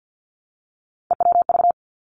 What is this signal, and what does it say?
Morse code, heard as a keyed tone, about 50 wpm, 700 Hz. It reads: EWV
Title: Three Challenging Letter-Number Combinations 50wpm